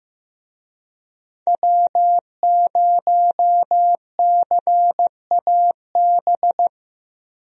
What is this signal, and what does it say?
Morse code, heard as a keyed tone, about 15 wpm, 700 Hz. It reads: W0CAB
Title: USA Call Signs - 5 Characters - Repeated 3x - 15wpm